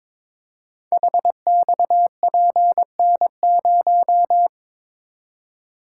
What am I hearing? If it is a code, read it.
Morse code, heard as a keyed tone, about 22 wpm, 700 Hz. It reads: HXPN0
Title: Code Groups - 5 Letter:Numbers - 22wpm